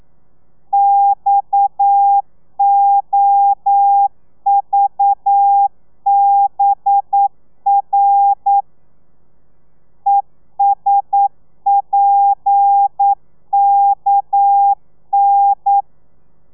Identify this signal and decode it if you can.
Morse code, heard as a keyed tone, about 9 wpm, 800 Hz. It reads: XOVBR ESPKN
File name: telegrafia / i2tfj / Lez7 40